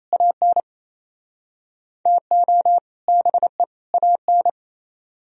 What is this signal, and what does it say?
Morse code, heard as a keyed tone, about 37 wpm, 700 Hz. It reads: AN TO BE AN